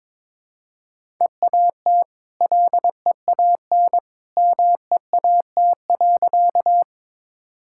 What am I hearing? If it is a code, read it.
Morse code, heard as a keyed tone, about 21 wpm, 690 Hz. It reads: EAT LEAN MEAT.
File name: practice / 22wpm / Easy Sentences with Hints - 22wpm